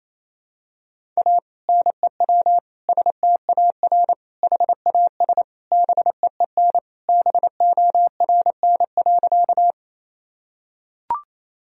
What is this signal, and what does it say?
Morse code, heard as a keyed tone, about 28 wpm, 700 Hz. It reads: A NEW STAR HAS BEEN BORN.
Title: Sentences from Top 1000 Words - Review Part 2 - 28wpm